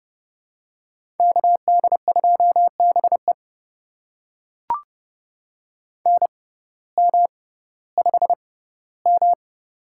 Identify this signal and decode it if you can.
Morse code, heard as a keyed tone, about 32 wpm, spaced as for 17 wpm, 700 Hz. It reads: KD2BE NM5M
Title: USA Call Signs - 6x Character Spacing - 30wpm